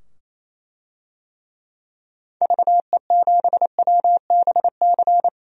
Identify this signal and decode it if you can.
Morse code, heard as a keyed tone, about 28 wpm, 700 Hz. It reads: VE7WBC
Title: DXCC-100 Call Signs - 6 Characters - 28wpm